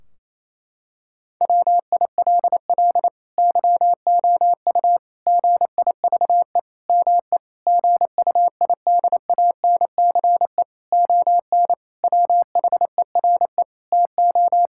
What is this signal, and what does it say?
Morse code, heard as a keyed tone, about 28 wpm, 700 Hz. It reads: WILL YOU GIVE ME GUIDANCE ON WHERE TO
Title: Sentences from Top 1600 Words 28wpm